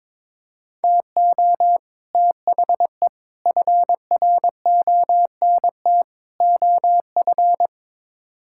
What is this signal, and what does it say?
Morse code, heard as a keyed tone, about 22 wpm, 700 Hz. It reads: TO THE FRONT OF
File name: practice / 22wpm / Build A Phrase - 5-Word Phrases - 22wpm